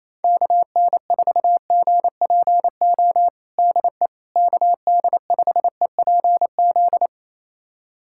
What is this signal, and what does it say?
Morse code, heard as a keyed tone, about 27 wpm, 700 Hz. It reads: KN4GPO DE KD5EPZ
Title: USA Callsign DE Callsign - 6 Characters - 28wpm